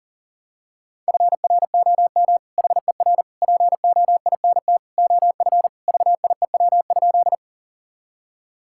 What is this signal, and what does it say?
Morse code, heard as a keyed tone, about 40 wpm, 700 Hz. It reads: FROM HER POINT OF VIEW?
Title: Sentences from Top 1000 Words - Review Part 6 - 40wpm